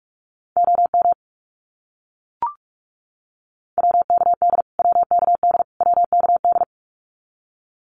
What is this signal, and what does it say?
Morse code, heard as a keyed tone, about 44 wpm, 700 Hz. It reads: OM WKD WKD WKD
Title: Abbreviations Repeated 3x - 45wpm